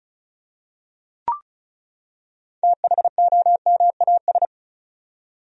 Morse code, heard as a keyed tone, about 36 wpm, 700 Hz. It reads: THOMAS
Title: USA Common Names - 35wpm